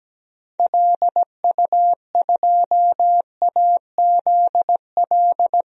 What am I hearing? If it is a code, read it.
Morse code, heard as a keyed tone, about 17 wpm, 700 Hz. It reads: LU2AZL
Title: DXCC-100 Call Signs - 6 Characters - 17wpm